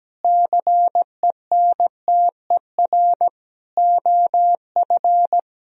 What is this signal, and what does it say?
Morse code, heard as a keyed tone, about 17 wpm, 700 Hz. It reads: CENTER OF
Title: Sentences from Top 1000 Words - Review Part 2 - 17wpm